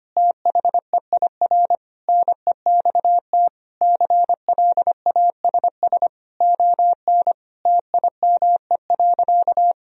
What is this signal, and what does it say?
Morse code, heard as a keyed tone, about 25 wpm, 700 Hz. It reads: THEIR NEXT CLASS ON TIME.
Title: Sentences from Top 1000 Words - Review Part 2 - 25wpm